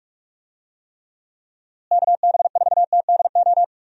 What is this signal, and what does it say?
Morse code, heard as a keyed tone, about 46 wpm, 700 Hz. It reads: KB4TDO
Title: USA Call Signs - 6 Characters - Repeated 3x - 45wpm